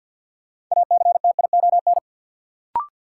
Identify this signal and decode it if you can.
Morse code, heard as a keyed tone, about 51 wpm, 700 Hz. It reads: ACTION
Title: Sentences from Top 700 Words - New Word Drill - 50wpm